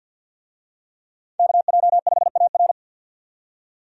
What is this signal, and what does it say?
Morse code, heard as a keyed tone, about 49 wpm, 700 Hz. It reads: KJ5AR
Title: USA Call Signs - 5 Characters - 50wpm